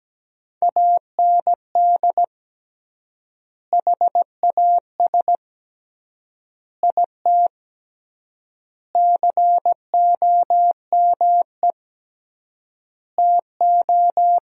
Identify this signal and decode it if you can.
Morse code, heard as a keyed tone, about 17 wpm, 700 Hz. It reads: AND HAS IT COME TO
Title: 3x Word-Spacing - Sentences from Top 100 Words - 17wpm